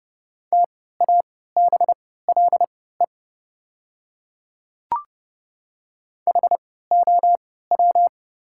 Morse code, heard as a keyed tone, about 30 wpm, spaced as for 17 wpm, 690 Hz. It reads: TABLE HOW
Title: 3x Character Spacing - Top 500 Words - 30wpm